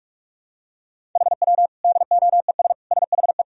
Morse code, heard as a keyed tone, about 47 wpm, 700 Hz. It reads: HW DOES SHE
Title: Sentences from Top 200 Words and Abbreviations 45wpm